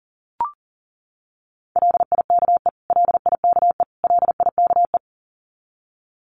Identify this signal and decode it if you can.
Morse code, heard as a keyed tone, about 40 wpm, 700 Hz. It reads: LIKE LIKE LIKE
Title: Top 500 Words Repeated 40wpm